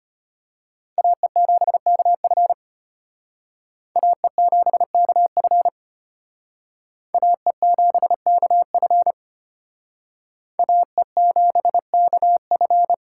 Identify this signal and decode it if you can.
Morse code, heard as a keyed tone, about 32 wpm, 700 Hz. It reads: AE7KF AE7KF AE7KF AE7KF